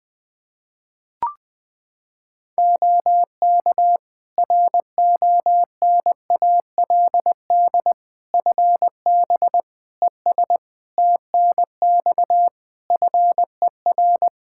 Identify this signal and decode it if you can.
Morse code, heard as a keyed tone, about 20 wpm, 700 Hz. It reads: OK RONALD FB ES TNX FER
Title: Practice QSO - Exchange 2 - Info - 20wpm